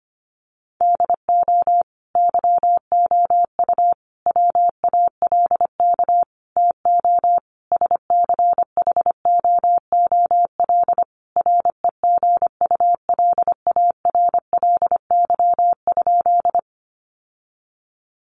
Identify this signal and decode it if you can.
Morse code, heard as a keyed tone, about 25 wpm, 700 Hz. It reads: DO YOU WALK TO SCHOOL REGULARLY?
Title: Sentences from Top 1400 Words 25wpm